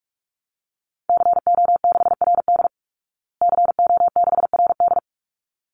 Morse code, heard as a keyed tone, about 44 wpm, 700 Hz. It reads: CO6RD CO6RD
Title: Call Signs Repeated 3x - 45wpm